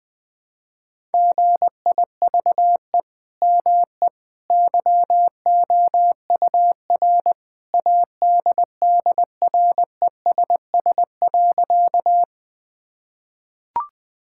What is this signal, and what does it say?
Morse code, heard as a keyed tone, about 20 wpm, 700 Hz. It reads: GIVE ME YOUR ADDRESS.
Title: Sentences from Top 1000 Words - Review Part 6 - 20wpm